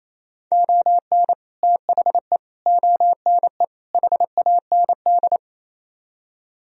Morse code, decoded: ON THE ONE HAND

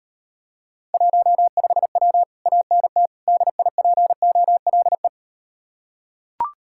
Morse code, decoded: 15W ANT DIPOLE

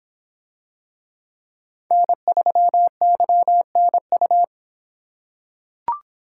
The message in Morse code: N3YNU

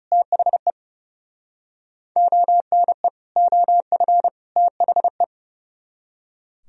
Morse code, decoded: THE ONE OF THE

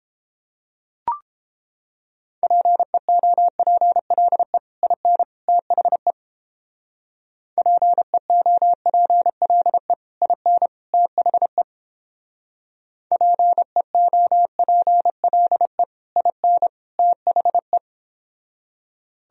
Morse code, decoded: PEOPLE IN THE PEOPLE IN THE PEOPLE IN THE

